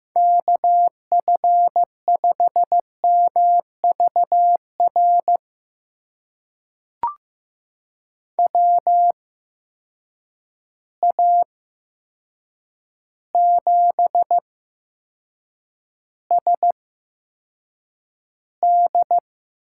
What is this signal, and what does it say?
Morse code, heard as a keyed tone, about 15 wpm, spaced as for 7 wpm, 700 Hz. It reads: KF5MVR WA7SD
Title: USA Call Signs - 6 Characters - 8x Character Spacing - 15wpm